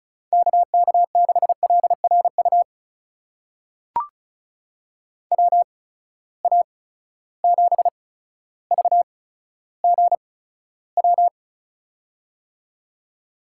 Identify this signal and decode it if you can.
Morse code, heard as a keyed tone, about 36 wpm, spaced as for 15 wpm, 700 Hz. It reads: KK6LRU WA7VGW
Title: USA Call Signs - 6 Characters - 8x Character Spacing - 35wpm